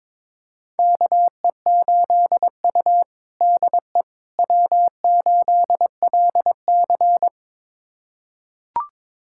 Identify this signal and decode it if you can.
Morse code, heard as a keyed tone, about 22 wpm, 700 Hz. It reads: KE8U DE W8LC
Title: USA Callsign DE Callsign - 4 Characters - 22wpm